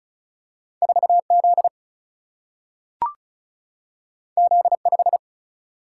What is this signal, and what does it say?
Morse code, heard as a keyed tone, about 36 wpm, 700 Hz. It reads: 4Z Z5
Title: Single-Letter Single-Number 35wpm